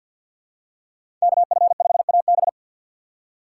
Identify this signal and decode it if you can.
Morse code, heard as a keyed tone, about 51 wpm, 700 Hz. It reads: KF5AB
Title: USA Call Signs - 5 Characters - 2x Character Spacing - 50wpm